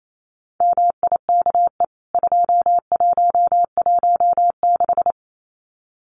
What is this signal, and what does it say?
Morse code, heard as a keyed tone, about 28 wpm, 700 Hz. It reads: MIKE 2116
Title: CWT Exchange 28wpm